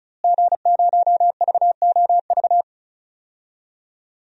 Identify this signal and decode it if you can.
Morse code, heard as a keyed tone, about 34 wpm, 700 Hz. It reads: G0VOV
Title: DXCC-100 Call Signs - 5 Characters - Repeated 2x - 35wpm